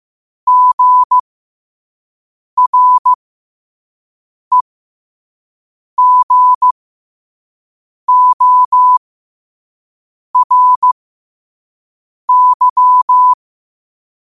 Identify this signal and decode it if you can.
Morse code, heard as a keyed tone, about 15 wpm, spaced as for 5 wpm, 1000 Hz. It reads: GREGORY
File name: qotdfiles / 5wpm